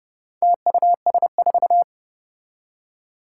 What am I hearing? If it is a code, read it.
Morse code, heard as a keyed tone, about 30 wpm, 700 Hz. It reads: TUS4
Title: Four Challenging Letter-Number Combinations 30wpm